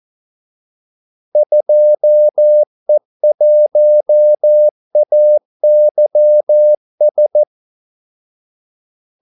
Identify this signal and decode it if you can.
Morse code, heard as a keyed tone, about 14 wpm, 600 Hz. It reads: 2E1AYS
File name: Callsigns-14wpm